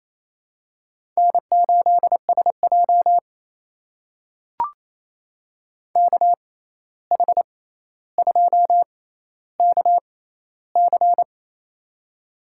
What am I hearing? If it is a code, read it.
Morse code, heard as a keyed tone, about 28 wpm, spaced as for 13 wpm, 700 Hz. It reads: N8SJ KH2KC
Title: USA Call Signs - 6x Character Spacing - 28wpm